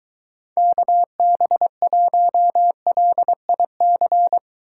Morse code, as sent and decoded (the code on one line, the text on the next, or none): -.- -... .---- .-.. .. -.-.
KB1LIC